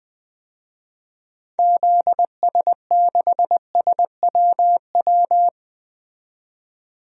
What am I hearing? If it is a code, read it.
Morse code, heard as a keyed tone, about 21 wpm, 700 Hz. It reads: ZS6SWW
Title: DXCC-100 Call Signs - 6 Characters - 20wpm